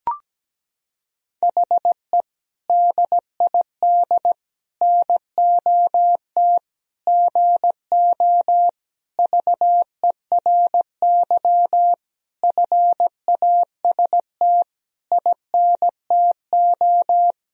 Morse code, heard as a keyed tone, about 17 wpm, 700 Hz. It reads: HE DID NOT GO VERY FAST INTO
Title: Sentences from Top 1000 Words - Review Part 2 - 17wpm